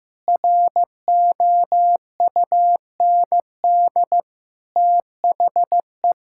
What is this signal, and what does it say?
Morse code, heard as a keyed tone, about 15 wpm, 700 Hz. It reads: ROUND THE
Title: Sentences from Top 1600 Words 15wpm